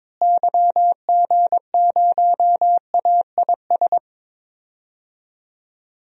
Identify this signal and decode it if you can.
Morse code, heard as a keyed tone, about 22 wpm, 700 Hz. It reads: YG0AIS
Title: DXCC-100 Call Signs - 6 Characters - Repeated 3x - 22wpm